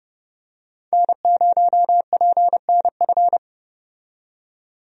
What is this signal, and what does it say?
Morse code, heard as a keyed tone, about 32 wpm, 700 Hz. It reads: N0PNF
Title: USA Call Signs - 5 Characters - 30wpm